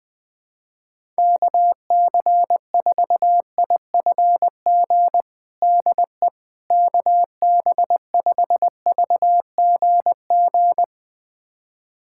Morse code, heard as a keyed tone, about 21 wpm, 700 Hz. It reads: KC4IFG DE KB5VGG
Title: USA Callsign DE Callsign - 6 Characters - 20wpm